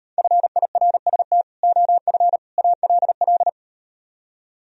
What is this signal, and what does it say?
Morse code, heard as a keyed tone, about 38 wpm, 700 Hz. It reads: FIRST OF ALL